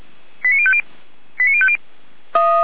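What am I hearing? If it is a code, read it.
Five-tone selective call (ZVEI), heard as quick runs of five tones, tones of about 70 ms. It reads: 89050 89050
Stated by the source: ZVEI five-tone selective call received on an SDR.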